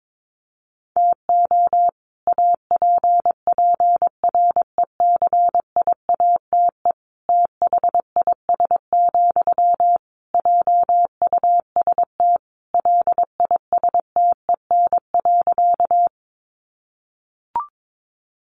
Morse code, heard as a keyed tone, about 22 wpm, 700 Hz. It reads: TO APPRECIATE THIS, JUST LISTEN.
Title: Sentences from Top 1300 Words 22wpm